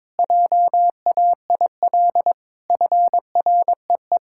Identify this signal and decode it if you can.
Morse code, heard as a keyed tone, about 22 wpm, 700 Hz. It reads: JAIL FREE
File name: Sentences with 7-Word Phrases - 22wpm